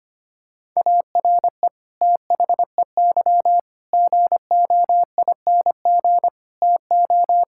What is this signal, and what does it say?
Morse code, heard as a keyed tone, about 25 wpm, 700 Hz. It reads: ARE THEY GOING TO
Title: Sentences from Top 1600 Words 25wpm